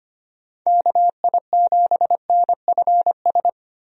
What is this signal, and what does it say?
Morse code, heard as a keyed tone, about 25 wpm, 700 Hz. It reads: KI7NFS
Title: USA Call Signs - 6 Characters - Repeated 2x - 25wpm